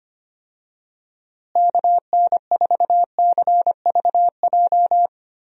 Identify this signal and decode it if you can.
Morse code, heard as a keyed tone, about 25 wpm, 700 Hz. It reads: KN4CVJ
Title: USA Call Signs - 6 Characters - Repeated 2x - 25wpm